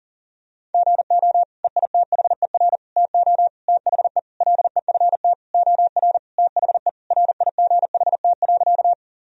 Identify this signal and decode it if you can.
Morse code, heard as a keyed tone, about 40 wpm, 700 Hz. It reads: GO EITHER TO THE LEFT OR THE RIGHT.
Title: Sentences from Top 1000 Words - Review Part 4 - 40wpm